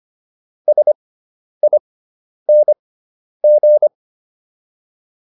Morse code, heard as a keyed tone, about 25 wpm, spaced as for 10 wpm, 600 Hz. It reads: SING